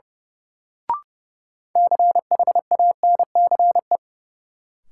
Morse code, heard as a keyed tone, about 30 wpm, 700 Hz. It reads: CHANCE
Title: Sets of 3 Words - New Word Drill - 30wpm